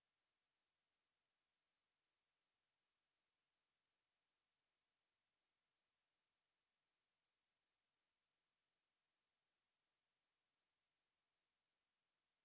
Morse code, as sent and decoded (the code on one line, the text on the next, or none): none